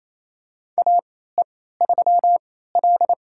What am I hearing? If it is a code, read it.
Morse code, heard as a keyed tone, about 28 wpm, spaced as for 16 wpm, 700 Hz. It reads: AE3L
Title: USA Call Signs - 3x Character Spacing - 28wpm